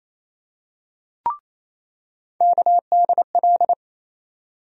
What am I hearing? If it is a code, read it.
Morse code, heard as a keyed tone, about 28 wpm, 700 Hz. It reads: KDL
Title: USA Call Sign Postfixes - 3 Letters 28wpm